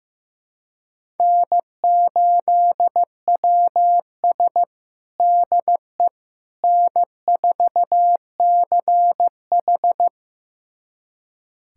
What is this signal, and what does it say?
Morse code, heard as a keyed tone, about 15 wpm, 700 Hz. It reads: N8WS DE N4CH
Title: USA Callsign DE Callsign - 4 Characters - 15wpm